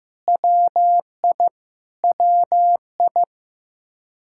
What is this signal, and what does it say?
Morse code, heard as a keyed tone, about 15 wpm, 700 Hz. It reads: WI WI
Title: Practice QSO - Exchange 1 - Beginning - 15wpm